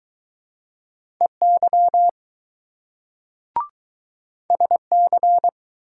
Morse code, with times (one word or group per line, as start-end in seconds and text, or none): 1.2-2.1 EY
4.5-5.5 SC